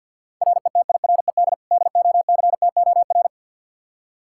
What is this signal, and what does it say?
Morse code, heard as a keyed tone, about 47 wpm, 700 Hz. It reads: RETIRED DOCTOR
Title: Practice QSO - Exchange 3 - More Info - 50wpm